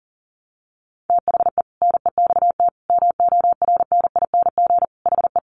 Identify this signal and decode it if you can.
Morse code, heard as a keyed tone, about 37 wpm, 700 Hz. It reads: THE NEXT MORNING HE
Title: Sentences from Top 600 Words 40wpm